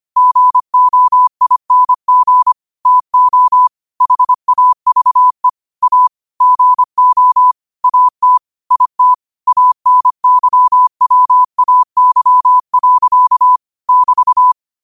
Morse code, heard as a keyed tone, about 25 wpm, 1000 Hz. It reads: GOING TO HAVE A GO AT IT ANYWAY. <BT>
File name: qotdfiles / 25wpm